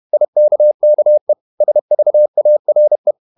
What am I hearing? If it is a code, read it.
Morse code, heard as a keyed tone, about 31 wpm, 600 Hz. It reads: IKKE SVARE
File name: Ferie QRQ 31wpm